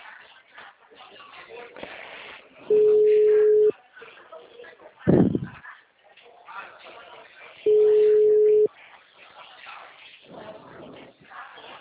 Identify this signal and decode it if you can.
Ringback tone (Europe, 425 Hz), with speech.